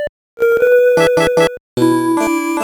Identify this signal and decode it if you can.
There is no code.